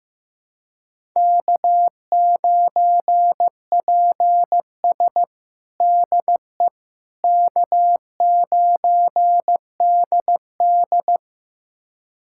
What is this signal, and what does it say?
Morse code, heard as a keyed tone, about 15 wpm, 700 Hz. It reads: K9PS DE K9DD
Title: USA Callsign DE Callsign - 4 Characters - 15wpm